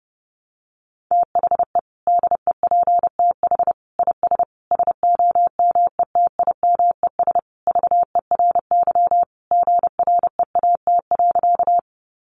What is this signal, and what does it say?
Morse code, heard as a keyed tone, about 28 wpm, 700 Hz. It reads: THE DEPTH IS SOMETIMES VERY GREAT.
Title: Sentences from Top 1200 Words 30wpm